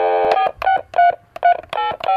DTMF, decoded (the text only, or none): 468